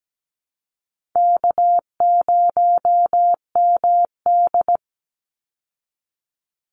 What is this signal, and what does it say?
Morse code, heard as a keyed tone, about 17 wpm, 700 Hz. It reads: K0MD
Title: Call Signs Repeated 3x - 17wpm